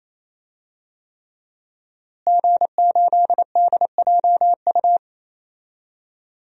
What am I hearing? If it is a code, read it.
Morse code, heard as a keyed tone, about 28 wpm, 700 Hz. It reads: G8DJU
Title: DXCC-100 Call Signs - 5 Characters - Repeated 3x - 28wpm